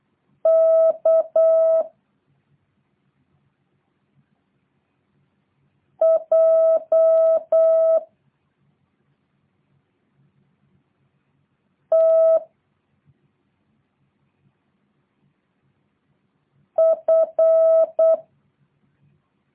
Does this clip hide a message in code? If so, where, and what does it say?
Yes: Morse code from 0.5 to 18.1 s, heard as a keyed tone, about 8 wpm, 650 Hz. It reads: K J T F